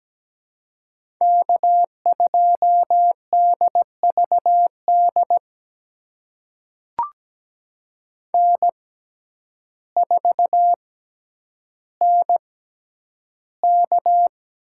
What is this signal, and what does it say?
Morse code, heard as a keyed tone, about 17 wpm, spaced as for 9 wpm, 700 Hz. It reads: K2DVD N4NK
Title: USA Call Signs - 5 Characters - 6x Character Spacing - 17wpm